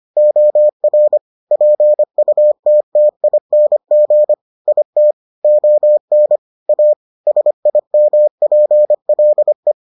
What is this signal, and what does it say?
Morse code, heard as a keyed tone, about 25 wpm, 600 Hz. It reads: OR PUTTING IT ON A SIMPLE